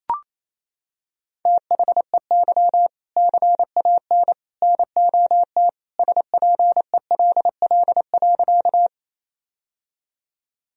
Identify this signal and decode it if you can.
Morse code, heard as a keyed tone, about 28 wpm, 700 Hz. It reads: THEY CAN NOT SPELL.
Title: Sentences from Top 1000 Words - Review Part 1 - 28wpm